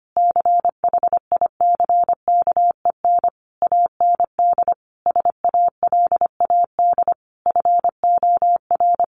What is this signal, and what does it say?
Morse code, heard as a keyed tone, about 25 wpm, 700 Hz. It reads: CHICKEN AND SALAD FOR
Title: Easy Phrases 25wpm